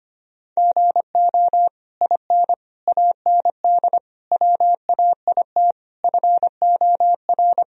Morse code, heard as a keyed tone, about 25 wpm, 700 Hz. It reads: GO IN AND WAIT FOR
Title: Sentences from Top 1000 Words - Review Part 2 - 25wpm